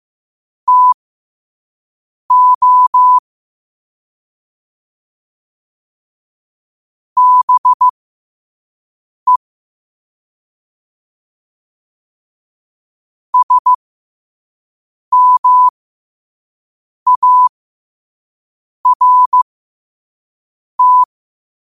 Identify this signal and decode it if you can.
Morse code, heard as a keyed tone, about 15 wpm, spaced as for 5 wpm, 1000 Hz. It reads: TO BE SMART
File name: qotdfiles / 5wpm